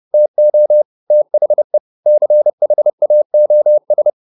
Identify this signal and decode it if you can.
Morse code, heard as a keyed tone, about 30 wpm, 600 Hz. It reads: TO THE CHAOS